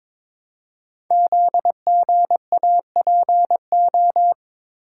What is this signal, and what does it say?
Morse code, heard as a keyed tone, about 22 wpm, 700 Hz. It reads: ZGAPO
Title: Code Groups - 5 Letters - 22wpm